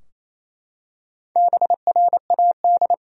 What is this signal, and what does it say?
Morse code, heard as a keyed tone, about 28 wpm, 700 Hz. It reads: BRAD
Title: USA Common Names - Max 4 Letters - 28wpm